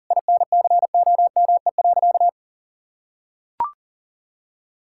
Morse code, heard as a keyed tone, about 40 wpm, 700 Hz. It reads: INCOME.
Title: Sentences from Top 1000 Words - Review Part 6 - 40wpm